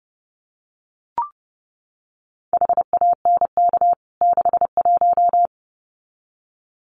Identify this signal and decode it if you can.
Morse code, heard as a keyed tone, about 30 wpm, 700 Hz. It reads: HANK 61
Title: CWT Exchange 30wpm